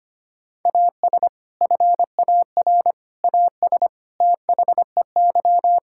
Morse code, heard as a keyed tone, about 25 wpm, 700 Hz. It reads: AS FAR AS THEY
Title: Sentences from Top 1000 Words - Review Part 6 - 25wpm